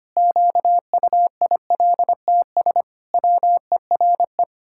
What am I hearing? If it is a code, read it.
Morse code, heard as a keyed tone, about 25 wpm, 700 Hz. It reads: QUILTS WERE
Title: Pangram Challenge v2 - 25wpm